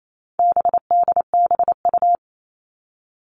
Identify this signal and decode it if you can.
Morse code, heard as a keyed tone, about 28 wpm, 700 Hz. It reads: BDBU
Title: All Cut Numbers - 4-Digit 28wpm